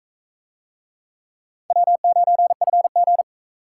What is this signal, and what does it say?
Morse code, heard as a keyed tone, about 36 wpm, 700 Hz. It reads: W9FG